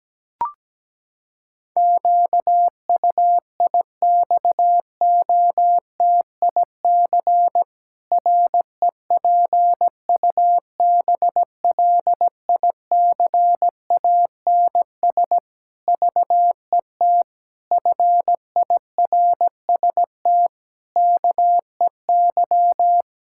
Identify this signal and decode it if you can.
Morse code, heard as a keyed tone, about 17 wpm, 700 Hz. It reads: QUIXOTIC REPUBLICANS VET FIRST KEY
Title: Pangram Challenge v2 - 17wpm